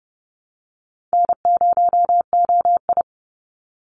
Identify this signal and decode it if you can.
Morse code, heard as a keyed tone, about 30 wpm, 700 Hz. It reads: N0OI
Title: NA SOTA Activator Callsigns 30wpm